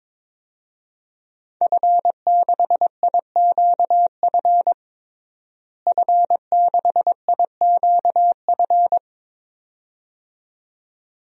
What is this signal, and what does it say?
Morse code, heard as a keyed tone, about 22 wpm, 700 Hz. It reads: F6IQF F6IQF
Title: DXCC-100 Call Signs - 5 Characters - Repeated 2x - 22wpm